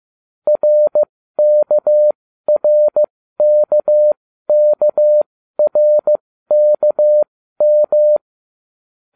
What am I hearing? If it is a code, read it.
Morse code, heard as a keyed tone, about 15 wpm, spaced as for 12 wpm, 600 Hz. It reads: RKRKKRKM